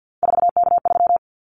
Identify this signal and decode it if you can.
Morse code, heard as a keyed tone, about 50 wpm, 700 Hz. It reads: 4K3
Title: Callsign Prefixes for Top 100 DXCC - 50wpm